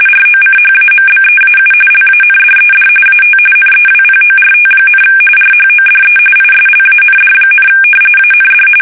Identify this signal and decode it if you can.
RTTY, about 45.45 baud. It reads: ROCHAMBEAU LANDED AT NEWPORT, RHODE ISLAND, AND
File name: fsk850 / usnavy1